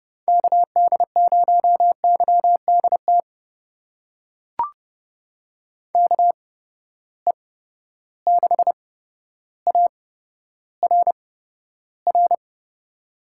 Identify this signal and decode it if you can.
Morse code, heard as a keyed tone, about 30 wpm, spaced as for 13 wpm, 700 Hz. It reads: KD0YDT KE6ARR
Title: USA Call Signs - 6 Characters - 8x Character Spacing - 30wpm